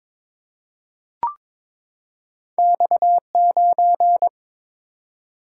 Morse code, heard as a keyed tone, about 22 wpm, 700 Hz. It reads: X9
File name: practice / 22wpm / Two Letter-Number Combinations - 22wpm